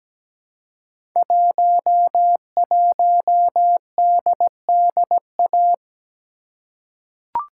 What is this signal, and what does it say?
Morse code, heard as a keyed tone, about 17 wpm, 700 Hz. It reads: 11DDA
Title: Code Groups - 5 Letter:Numbers - 17wpm